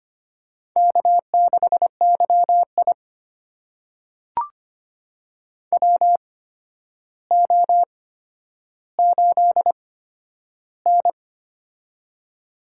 Morse code, heard as a keyed tone, about 25 wpm, spaced as for 11 wpm, 700 Hz. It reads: K6YI WO8N
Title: USA Call Signs - 4 Characters - 8x Character Spacing - 25wpm